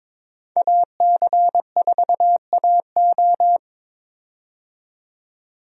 Morse code, heard as a keyed tone, about 22 wpm, 700 Hz. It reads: AC4AO
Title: USA Call Signs - 5 Characters - 22wpm